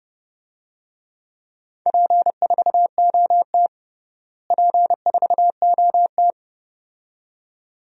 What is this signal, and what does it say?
Morse code, heard as a keyed tone, about 30 wpm, 700 Hz. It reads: P4OT P4OT
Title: DXCC-100 Call Signs - 4 Characters - Repeated 2x - 30wpm